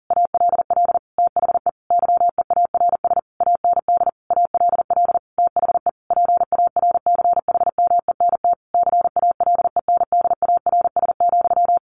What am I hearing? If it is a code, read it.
Morse code, heard as a keyed tone, about 40 wpm, 700 Hz. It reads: ALL THE YEARS AND ALL THE PARCHMENT CALENDARS,
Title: A Sound Of Thunder 40wpm